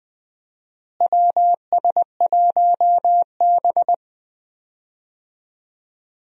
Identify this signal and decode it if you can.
Morse code, heard as a keyed tone, about 20 wpm, 700 Hz. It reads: WS1B
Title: USA Call Signs - 4 Characters - Repeated 2x - 20wpm